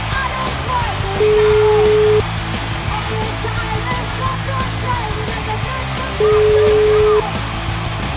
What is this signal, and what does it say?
Ringback tone (Europe, 425 Hz), with noise.